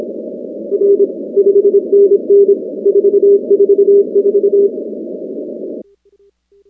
Morse code, heard as a keyed tone, about 26 wpm, 400 Hz. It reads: R 5NN 444
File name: contests / 2019-cw-wpx-kt5j-k5pi- -2330 0-19